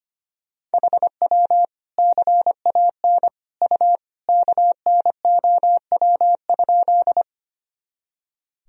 Morse code, heard as a keyed tone, about 25 wpm, 700 Hz. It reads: HW CAN U KNOW?
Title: Sentences from Top 100 Words and Abbreviations 25wpm